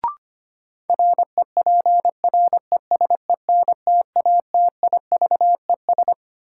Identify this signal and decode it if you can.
Morse code, heard as a keyed tone, about 25 wpm, 700 Hz. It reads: REPRESENTATIVES
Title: Sets of 5 Words Encore - New Word Drill - 25wpm